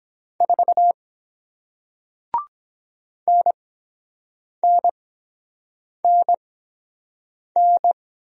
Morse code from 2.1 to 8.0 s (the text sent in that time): NNNN